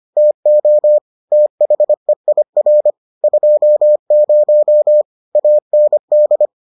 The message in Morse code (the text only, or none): TO THEIR 20 AND